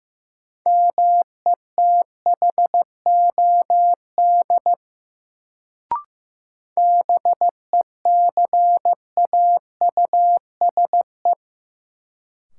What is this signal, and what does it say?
Morse code, heard as a keyed tone, about 15 wpm, 700 Hz. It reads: METHOD BECAUSE
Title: Sentences from Top 600 Words - New Word Drill - 15wpm